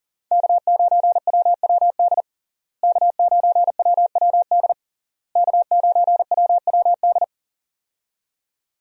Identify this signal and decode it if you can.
Morse code, heard as a keyed tone, about 37 wpm, 700 Hz. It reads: K9WWD K9WWD K9WWD
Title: USA Call Signs - 5 Characters - Repeated 3x - 40wpm